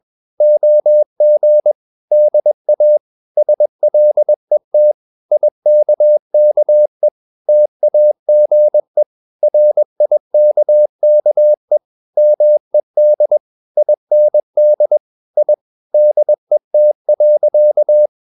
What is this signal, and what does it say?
Morse code, heard as a keyed tone, about 21 wpm, 600 Hz. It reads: OG DA SLET IKKE TAGE RIKKE MED IND I DET.
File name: Ferie QRS 21wpm